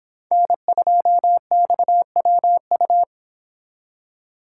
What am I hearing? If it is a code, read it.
Morse code, heard as a keyed tone, about 25 wpm, 700 Hz. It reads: N2XWU